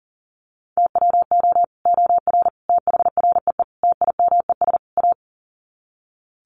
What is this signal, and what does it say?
Morse code, heard as a keyed tone, about 37 wpm, 700 Hz. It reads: TWO OR THREE TIMES A
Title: Sets of 5 Words Encore2 40wpm